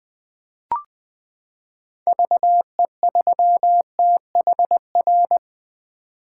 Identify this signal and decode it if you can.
Morse code, heard as a keyed tone, about 20 wpm, 700 Hz. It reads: VE3THR
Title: 2023 POTA Hunter Call Signs - 20wpm